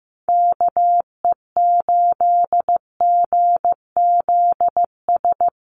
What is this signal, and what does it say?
Morse code, heard as a keyed tone, about 15 wpm, 700 Hz. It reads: KE8GZS
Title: USA Call Signs Repeated 3x 15wpm